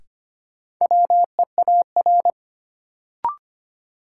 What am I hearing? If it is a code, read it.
Morse code, heard as a keyed tone, about 25 wpm, 700 Hz. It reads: WEAR
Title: Sentences from Top 1000 Words - Review Part 6 - New Word Drill - 25wpm